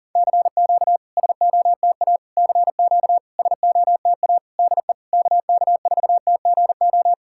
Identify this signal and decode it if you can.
Morse code, heard as a keyed tone, about 44 wpm, 700 Hz. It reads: CQ SOTA CQ SOTA DE KK4TGO
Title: SOTA QSO Practice 40wpm